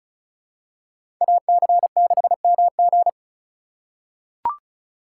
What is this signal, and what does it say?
Morse code, heard as a keyed tone, about 34 wpm, 700 Hz. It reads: AC6MG
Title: USA Call Signs - 5 Characters - 4x Character Spacing - 35wpm